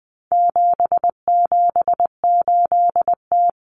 Morse code, heard as a keyed tone, about 20 wpm, 700 Hz. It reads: 778T
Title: Common Cut Numbers - 4-Digit 20wpm